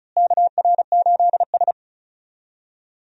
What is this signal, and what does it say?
Morse code, heard as a keyed tone, about 36 wpm, 700 Hz. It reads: KR8S
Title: USA Call Signs - 4 Characters - Repeated 3x - 35wpm